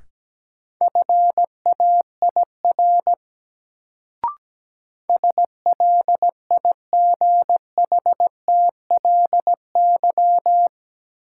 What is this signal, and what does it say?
Morse code, heard as a keyed tone, about 17 wpm, 700 Hz. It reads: FAIR SLIGHTLY
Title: Sentences from Top 1300 Words - New Word Drill Alt - 17wpm